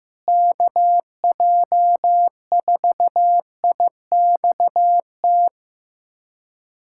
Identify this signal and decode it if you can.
Morse code, heard as a keyed tone, about 15 wpm, 700 Hz. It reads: KJ4IXT
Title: USA Callsign DE Callsign - 6 Characters - 15wpm